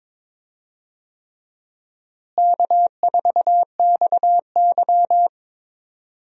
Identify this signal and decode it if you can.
Morse code, heard as a keyed tone, about 22 wpm, 700 Hz. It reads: K4XY
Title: USA Call Signs - 4 Characters - Repeated 3x - 22wpm